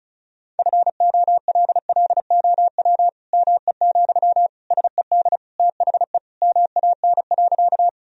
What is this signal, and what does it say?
Morse code, heard as a keyed tone, about 34 wpm, 700 Hz. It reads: FOLLOW ME, SED THE MAN.
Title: Sentences from Top 200 Words and Abbreviations 35wpm